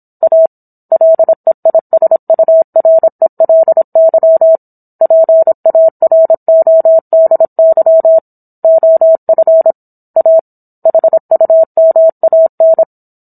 Morse code, heard as a keyed tone, about 27 wpm, 650 Hz. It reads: A LEISURELY PARODY OF A HUMAN
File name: Book One / 17-Chapter 17 - 26 WPM